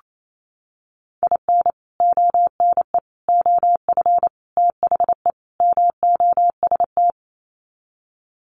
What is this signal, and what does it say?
Morse code, heard as a keyed tone, about 28 wpm, 700 Hz. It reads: IN ONE OF THE MOST
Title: Sets of 5 Words Encore 28wpm